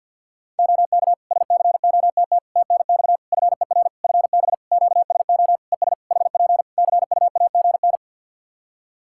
Morse code, heard as a keyed tone, about 51 wpm, 700 Hz. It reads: OK SCOTT TNX FER FB QSO ES HP CUAGN
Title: Practice QSO - Exchange 4 - Ending - 50wpm